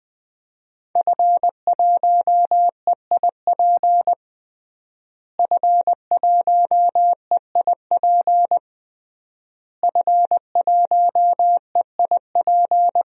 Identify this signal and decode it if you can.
Morse code, heard as a keyed tone, about 21 wpm, 700 Hz. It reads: F1EIP F1EIP F1EIP
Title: DXCC-100 Call Signs - 5 Characters - Repeated 3x - 20wpm